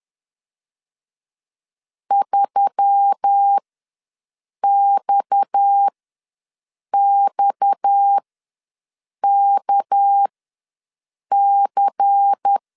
Morse code, heard as a keyed tone, about 10 wpm, spaced as for 6 wpm, 800 Hz. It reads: 3XXKC
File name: LEZIONI / 6 PKX0 / 6 PKX ws 10wpm 3706